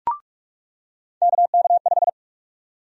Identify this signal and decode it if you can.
Morse code, heard as a keyed tone, about 45 wpm, 700 Hz. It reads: KK5
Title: USA Call Sign Prefixes - 45wpm